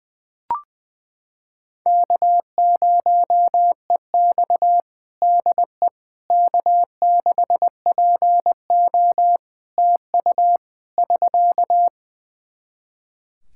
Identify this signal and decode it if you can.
Morse code, heard as a keyed tone, about 21 wpm, 700 Hz. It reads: K0EX DE K6PO TU <SK>
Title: Practice QSO - Exchange 4 - Ending - 20wpm